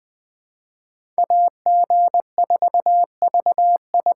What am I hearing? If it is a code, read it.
Morse code, heard as a keyed tone, about 20 wpm, 700 Hz. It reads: AG4VI
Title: USA Call Signs - 5 Characters - 3x Character Spacing - 20wpm